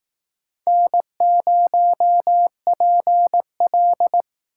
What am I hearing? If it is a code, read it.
Morse code, heard as a keyed tone, about 18 wpm, 700 Hz. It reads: N0PL
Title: Speed Racing - USA 1x2 Call Signs 15wpm